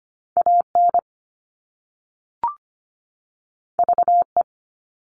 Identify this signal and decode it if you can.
Morse code, heard as a keyed tone, about 25 wpm, 700 Hz. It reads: AN VE
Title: All Cut Numbers - 2-Digit 25wpm